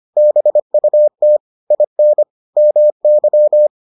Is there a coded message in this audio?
Yes: Morse code from 0.2 to 3.7 s, heard as a keyed tone, about 25 wpm, 600 Hz. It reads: BUT IN MY